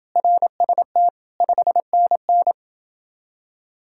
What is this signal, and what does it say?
Morse code, heard as a keyed tone, about 30 wpm, 700 Hz. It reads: RST 5NN